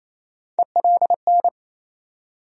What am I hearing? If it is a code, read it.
Morse code, heard as a keyed tone, about 28 wpm, 700 Hz. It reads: ELN
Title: USA Call Sign Postfixes - 3 Letters 28wpm